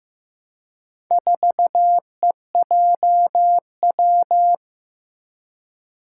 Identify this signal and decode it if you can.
Morse code, heard as a keyed tone, about 15 wpm, 700 Hz. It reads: 4EJW
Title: Four Challenging Letter-Number Combinations 15wpm